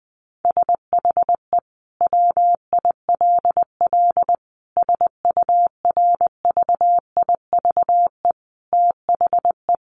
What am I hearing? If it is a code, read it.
Morse code, heard as a keyed tone, about 20 wpm, 700 Hz. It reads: SHE WILL SURVIVE THE
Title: Sentences from Top 1300 Words 20wpm